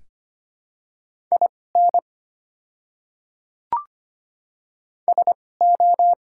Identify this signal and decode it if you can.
Morse code, heard as a keyed tone, about 25 wpm, spaced as for 18 wpm, 700 Hz. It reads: IN SO
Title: 2x Character Spacing - Top 100 Words - 25wpm